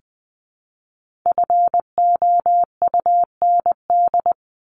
Morse code, heard as a keyed tone, about 20 wpm, 700 Hz. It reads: FOUND
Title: Top 300 Words Repeated 20wpm